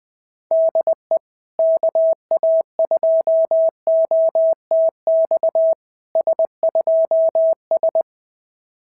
Morse code, heard as a keyed tone, about 21 wpm, 650 Hz. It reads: DE KA2OTX S2S
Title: SOTA QSO Practice 20wpm